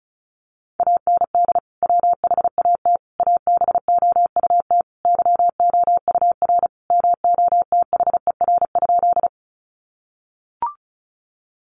Morse code, heard as a keyed tone, about 34 wpm, 700 Hz. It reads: AND WHAT ABOUT YOUR MOTHER?
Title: Sentences from Top 200 Words 35wpm